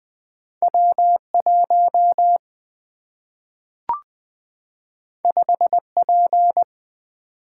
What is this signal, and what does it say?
Morse code, heard as a keyed tone, about 20 wpm, 700 Hz. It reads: W 1 5 P
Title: Single-Letter Single-Number 20wpm